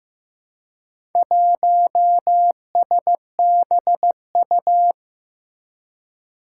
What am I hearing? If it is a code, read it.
Morse code, heard as a keyed tone, about 15 wpm, 700 Hz. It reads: 1SBU
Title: Four Challenging Letter-Number Combinations 15wpm